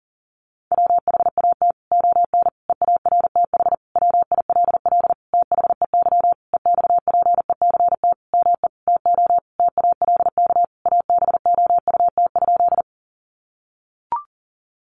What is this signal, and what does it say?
Morse code, heard as a keyed tone, about 40 wpm, 700 Hz. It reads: WHAT ON EARTH WILL THEY EXPECT ME TO TALK ABOUT?
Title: Sentences from Top 1100 Words 40wpm